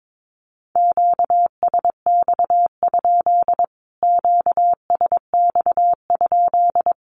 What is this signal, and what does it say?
Morse code, heard as a keyed tone, about 22 wpm, 700 Hz. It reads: QSX? QSX?
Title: Q-Codes Repeated 3x 22wpm